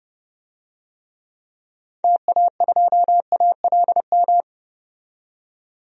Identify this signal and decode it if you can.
Morse code, heard as a keyed tone, about 31 wpm, 700 Hz. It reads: TA2ALM
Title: DXCC-100 Call Signs - 6 Characters - 30wpm